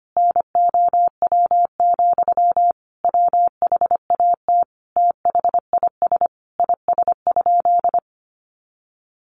Morse code, heard as a keyed tone, about 25 wpm, 700 Hz. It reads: NOW, WHAT THIS IS?
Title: Sentences from Top 1500 Words 25wpm